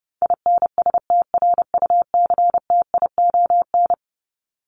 Morse code, heard as a keyed tone, about 30 wpm, 700 Hz. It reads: INSTRUCTION
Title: Top 1751-2000 Words 30wpm